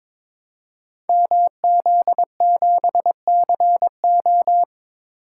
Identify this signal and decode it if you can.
Morse code, heard as a keyed tone, about 21 wpm, 700 Hz. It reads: MZ7CO
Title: Code Groups - 5 Letter:Numbers - 22wpm